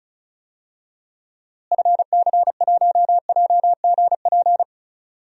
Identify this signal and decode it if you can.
Morse code, heard as a keyed tone, about 36 wpm, 700 Hz. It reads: FC1JGP
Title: DXCC-100 Call Signs - 6 Characters - Repeated 3x - 35wpm